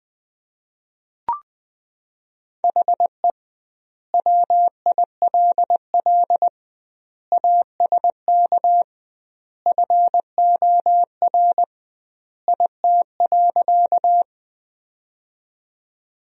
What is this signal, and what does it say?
Morse code, heard as a keyed tone, about 20 wpm, 700 Hz. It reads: HE WILL ASK FOR IT.
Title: 2x Word-Spacing - Sentences from Top 200 Words - 20wpm